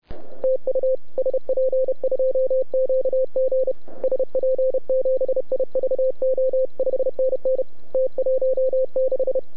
Morse code, heard as a keyed tone, about 32 wpm, 520 Hz. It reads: TU SP2QG SP7IVO 5NN T16